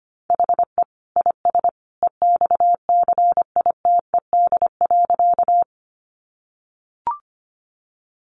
Morse code, heard as a keyed tone, about 25 wpm, 700 Hz. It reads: HE IS EXCITED.
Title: Additional Easy Phrases 25wpm